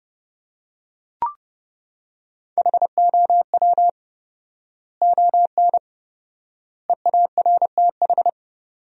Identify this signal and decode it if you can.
Morse code, heard as a keyed tone, about 30 wpm, 700 Hz. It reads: HOW ON EARTH
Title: 4x Word-Spacing - Sentences from Top 200 Words - 30wpm